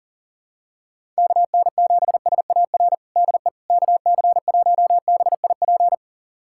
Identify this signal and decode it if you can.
Morse code, heard as a keyed tone, about 40 wpm, 700 Hz. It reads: KN7SAR DE KC1BIP
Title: USA Callsign DE Callsign 40wpm